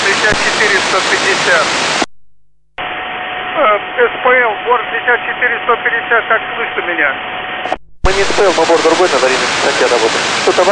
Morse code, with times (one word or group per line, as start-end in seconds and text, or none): none